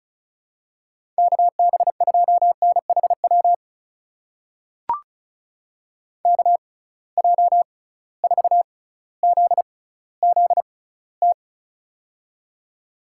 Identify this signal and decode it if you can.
Morse code, heard as a keyed tone, about 36 wpm, spaced as for 18 wpm, 700 Hz. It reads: KB2NHW KJ4ZZT
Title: USA Call Signs - 6 Characters - 6x Character Spacing - 35wpm